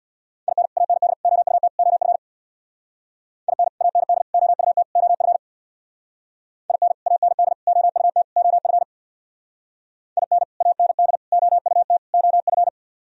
Morse code, heard as a keyed tone, about 64 wpm, 700 Hz. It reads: IN AND OUT OF IN AND OUT OF IN AND OUT OF IN AND OUT OF